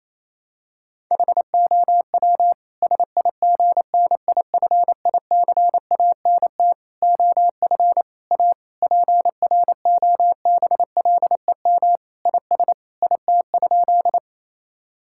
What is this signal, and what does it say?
Morse code, heard as a keyed tone, about 28 wpm, 700 Hz. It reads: HOW SIGNIFICANT OF A PROBLEM IS IT?
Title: Sentences from Top 1000 Words - Review Part 6 - 28wpm